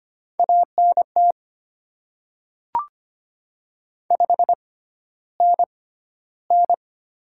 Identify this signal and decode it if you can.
Morse code, heard as a keyed tone, about 25 wpm, spaced as for 13 wpm, 700 Hz. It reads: ANT 5NN
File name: Top 20 QSO Elements - 6x Character Spacing - 25wpm